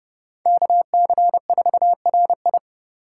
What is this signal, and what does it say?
Morse code, heard as a keyed tone, about 29 wpm, 700 Hz. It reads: KC4RI